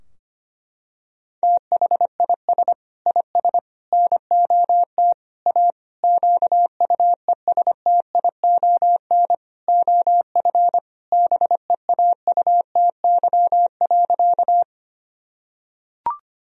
Morse code, heard as a keyed tone, about 25 wpm, 700 Hz. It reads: THIS IS NOT A QUESTION OF BEAUTY.
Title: Sentences from Top 1000 Words - Review Part 2 - 25wpm